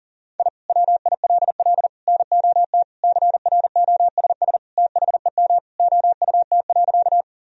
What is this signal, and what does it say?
Morse code, heard as a keyed tone, about 40 wpm, 700 Hz. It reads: I WILL NOT CROSS THEM OUT.
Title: Sentences from Top 1000 Words - Review Part 1 - 40wpm